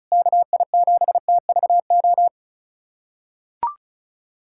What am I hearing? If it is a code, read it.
Morse code, heard as a keyed tone, about 36 wpm, 700 Hz. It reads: KI7TVO
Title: USA Callsign DE Callsign - 6 Characters - 35wpm